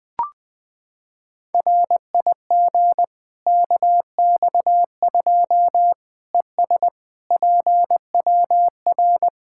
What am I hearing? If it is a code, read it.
Morse code, heard as a keyed tone, about 21 wpm, 700 Hz. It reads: RIG KX2 ES PWR
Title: K1USN SST QSO Practice 20wpm